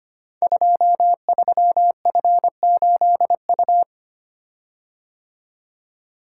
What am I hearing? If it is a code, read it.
Morse code, heard as a keyed tone, about 25 wpm, 700 Hz. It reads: 23F8U
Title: Code Groups - 5 Letter:Numbers - 25wpm